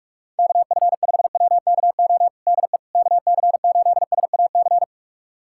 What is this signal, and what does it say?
Morse code, heard as a keyed tone, about 46 wpm, 700 Hz. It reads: KF5WKO DE KC8SAC
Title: USA Callsign DE Callsign 45wpm